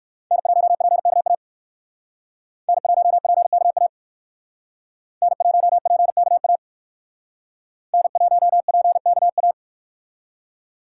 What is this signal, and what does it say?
Morse code, heard as a keyed tone, about 58 wpm, 700 Hz. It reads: N1PKA N1PKA N1PKA N1PKA